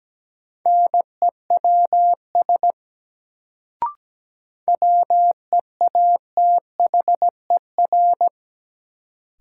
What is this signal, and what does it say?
Morse code, heard as a keyed tone, about 17 wpm, 700 Hz. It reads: NEWS WEATHER
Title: Sentences from Top 800 Words - New Word Drill - 17wpm